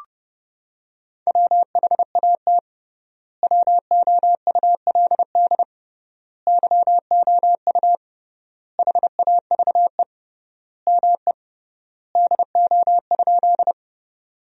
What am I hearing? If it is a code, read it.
Morse code, heard as a keyed tone, about 30 wpm, 700 Hz. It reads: WHAT WOULD YOU HAVE ME DO?
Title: 3x Word-Spacing - Sentences from Top 200 Words - 30wpm